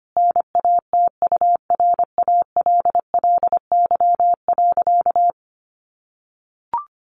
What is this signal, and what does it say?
Morse code, heard as a keyed tone, about 25 wpm, 700 Hz. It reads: NATURALLY.
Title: Sentences from Top 1300 Words 25wpm